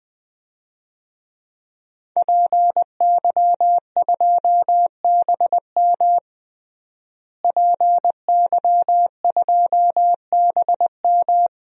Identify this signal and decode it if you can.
Morse code, heard as a keyed tone, about 20 wpm, 700 Hz. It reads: PY2BM PY2BM
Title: DXCC-100 Call Signs - 5 Characters - Repeated 3x - 20wpm